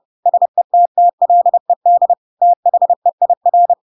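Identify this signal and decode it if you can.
Morse code, heard as a keyed tone, about 30 wpm, 700 Hz. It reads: SETTLED THEIR
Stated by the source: Chapter 2 of The Adventures of Pinocchio read as a book in Morse code (CW) at 30 wpm, by ik1hge.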